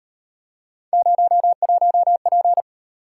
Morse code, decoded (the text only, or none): 01P